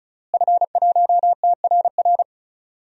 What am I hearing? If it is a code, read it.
Morse code, heard as a keyed tone, about 36 wpm, 700 Hz. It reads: F1TRR
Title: DXCC-100 Call Signs - 5 Characters - Repeated 3x - 35wpm